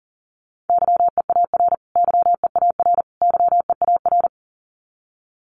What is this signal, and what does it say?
Morse code, heard as a keyed tone, about 40 wpm, 700 Hz. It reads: YEAR YEAR YEAR
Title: Top 500 Words Repeated 40wpm